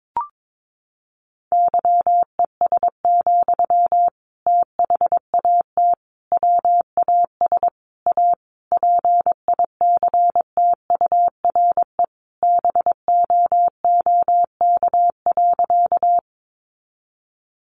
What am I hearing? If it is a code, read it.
Morse code, heard as a keyed tone, about 22 wpm, 700 Hz. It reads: YES, THAT WAS A PICTURE BOOK.
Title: Sentences from Top 500 Words 22wpm